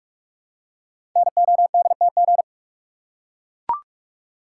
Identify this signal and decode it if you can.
Morse code, heard as a keyed tone, about 44 wpm, 700 Hz. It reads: NODTG
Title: Code Groups - 5 Letters - 45wpm